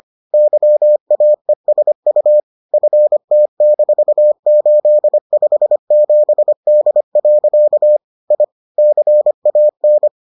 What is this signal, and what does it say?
Morse code, heard as a keyed tone, about 25 wpm, 600 Hz. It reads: YAESU FT-857D. I CAN